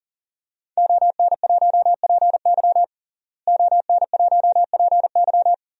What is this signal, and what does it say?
Morse code, heard as a keyed tone, about 40 wpm, 700 Hz. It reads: ON1PY ON1PY
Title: DXCC-100 Call Signs - 5 Characters - Repeated 2x - 40wpm